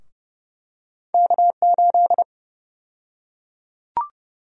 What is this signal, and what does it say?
Morse code, heard as a keyed tone, about 30 wpm, 700 Hz. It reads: K8
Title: USA Call Sign Prefixes - Repeated 2x - 30wpm